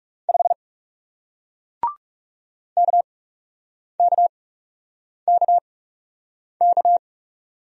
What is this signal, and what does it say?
Morse code, heard as a keyed tone, about 40 wpm, spaced as for 8 wpm, 700 Hz. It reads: 5 KKKK